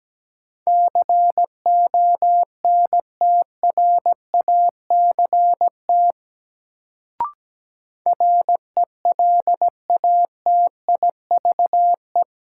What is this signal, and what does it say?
Morse code, heard as a keyed tone, about 17 wpm, 700 Hz. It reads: CONTRACT RELATIVE
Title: Sentences from Top 1400 Words - New Word Drill Alt - 17wpm